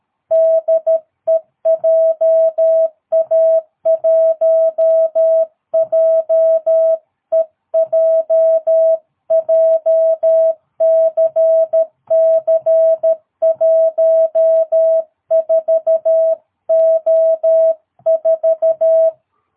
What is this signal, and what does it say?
Morse code, heard as a keyed tone, about 13 wpm, 660 Hz. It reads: DE JA1JEJ JCC 14O4
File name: jh8cbh / 2022cw